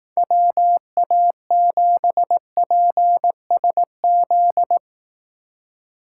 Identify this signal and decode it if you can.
Morse code, heard as a keyed tone, about 18 wpm, 700 Hz. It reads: WA7PSZ
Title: Speed Racing - USA 2x3 Call Signs 15wpm